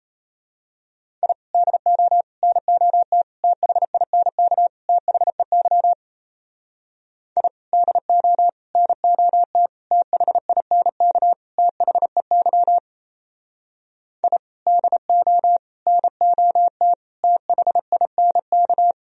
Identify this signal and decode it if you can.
Morse code, heard as a keyed tone, about 32 wpm, 700 Hz. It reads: I DO NOT THINK THEY I DO NOT THINK THEY I DO NOT THINK
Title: Speed-Racing - Sets of 5 Words 28wpm